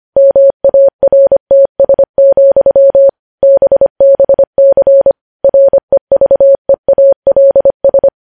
Morse code, heard as a keyed tone, about 25 wpm, 550 Hz. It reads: MARTS, BBC REVEALS
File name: podcast / News headlines-1x-25x25